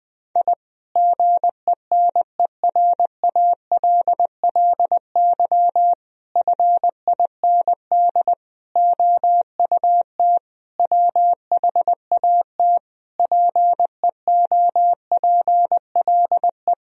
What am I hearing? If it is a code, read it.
Morse code, heard as a keyed tone, about 20 wpm, 700 Hz. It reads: I GENERALLY FIND OUT WHAT PEOPLE
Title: Sentences from Top 1000 Words - Review Part 4 - 20wpm